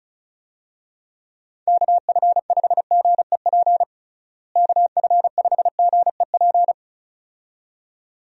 Morse code, heard as a keyed tone, about 36 wpm, 700 Hz. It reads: KF5GEP KF5GEP
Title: USA Call Signs - 6 Characters - Repeated 2x - 35wpm